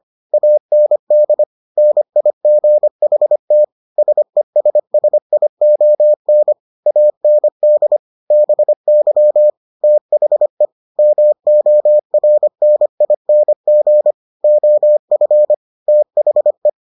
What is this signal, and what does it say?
Morse code, heard as a keyed tone, about 25 wpm, 600 Hz. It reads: AND NIGHT SESSION AND BY THE MORNING OF THE